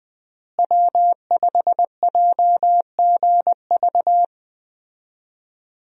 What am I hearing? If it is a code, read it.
Morse code, heard as a keyed tone, about 20 wpm, 700 Hz. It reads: W5JGV
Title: USA Call Signs - 5 Characters - Repeated 2x - 20wpm